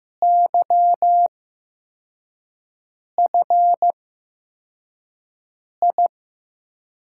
Morse code, heard as a keyed tone, about 15 wpm, spaced as for 4 wpm, 700 Hz. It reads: YFI
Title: USA Call Signs - 6 Characters - 8x Character Spacing - 15wpm